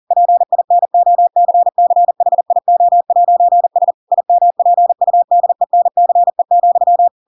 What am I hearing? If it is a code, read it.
Morse code, heard as a keyed tone, about 40 wpm, 700 Hz. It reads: PINOCCHIO'S IMPUDENCE,
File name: Pinocchio / Pinocchio ch4 40 40 ik1hge